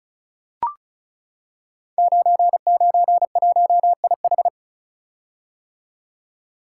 Morse code, heard as a keyed tone, about 36 wpm, 700 Hz. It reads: 991IH
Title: Code Groups - 5 Letter:Numbers - 35wpm